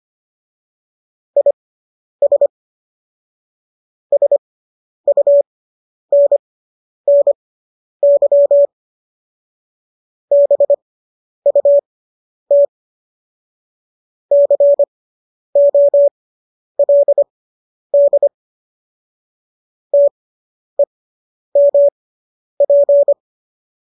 Morse code, heard as a keyed tone, about 25 wpm, spaced as for 10 wpm, 570 Hz. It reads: IS SUNNY BUT COLD TEMP